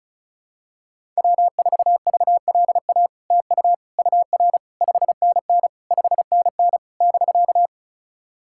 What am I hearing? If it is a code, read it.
Morse code, heard as a keyed tone, about 36 wpm, 700 Hz. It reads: W4VLA TU UR 5NN 5NN <BK>
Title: SOTA QSO Practice 35wpm